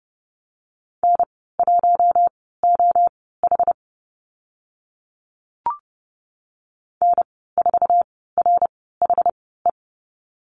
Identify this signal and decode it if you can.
Morse code, heard as a keyed tone, about 30 wpm, spaced as for 17 wpm, 700 Hz. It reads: N1OH N4RHE
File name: Lesson 14g - Number 2 - Call Signs 30wpm